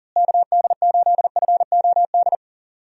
Morse code, heard as a keyed tone, about 40 wpm, 700 Hz. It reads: KD8FOD
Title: USA Call Signs - 6 Characters - Repeated 2x - 40wpm